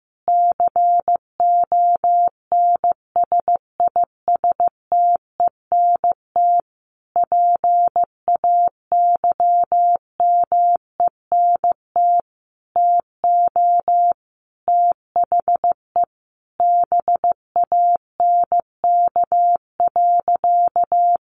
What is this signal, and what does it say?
Morse code, heard as a keyed tone, about 15 wpm, 700 Hz. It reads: CONSISTENT PAYMENT TO THE BANK.
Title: Sentences from Top 1500 Words 15wpm